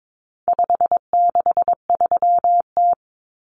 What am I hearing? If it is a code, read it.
Morse code, heard as a keyed tone, about 22 wpm, 700 Hz. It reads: 563T
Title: Common Cut Numbers - 4-Digit 22wpm